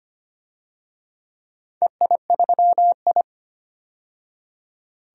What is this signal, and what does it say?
Morse code, heard as a keyed tone, about 25 wpm, 700 Hz. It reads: EI3I
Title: DXCC-100 Call Signs - 4 Characters - 25wpm